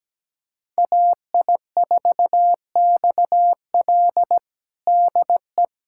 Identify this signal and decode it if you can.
Morse code, heard as a keyed tone, about 17 wpm, 700 Hz. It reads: AI4XL DE
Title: USA Callsign DE Callsign - 5 Characters - 17wpm